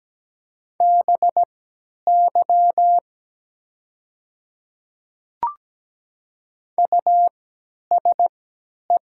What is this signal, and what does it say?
Morse code, heard as a keyed tone, about 17 wpm, spaced as for 10 wpm, 700 Hz. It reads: BY USE
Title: 3x Character Spacing - Top 100 Words - 17wpm